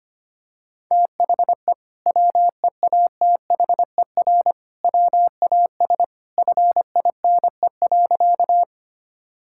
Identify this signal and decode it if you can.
Morse code, heard as a keyed tone, about 25 wpm, 700 Hz. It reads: THE WEATHER WAS FINE.
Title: Sentences from Top 1000 Words - Review Part 5 - 25wpm